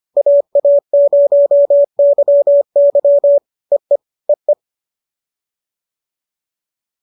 Morse code, heard as a keyed tone, about 25 wpm, 570 Hz. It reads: AA0YY EE EE